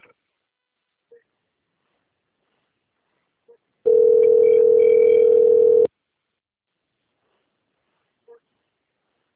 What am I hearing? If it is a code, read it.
Ringback tone (North America), 440 + 480 Hz.
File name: rg-1600-unknown- - - .29860